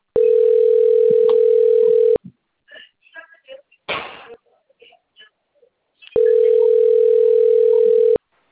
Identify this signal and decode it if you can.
Ringback tone (North America), 440 + 480 Hz.
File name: out- - - - - .12142